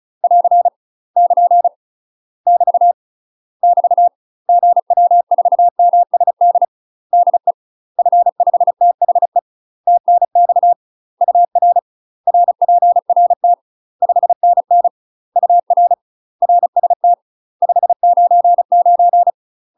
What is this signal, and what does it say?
Morse code, heard as a keyed tone, about 34 wpm, 700 Hz. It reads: <AR> <KN> <BT> <BT> GW4MSD DE F5THE TNX UR RPRT 5NN UR RST 599
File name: UK MASTER QSOs TRAINING / 35WPM / UK MASTER QSOs 0001